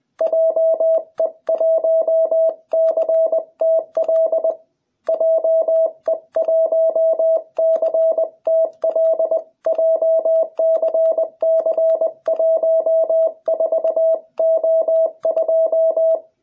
Morse code, heard as a keyed tone, about 20 wpm, 660 Hz. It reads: JE1CTL JE1CTL JCC14O2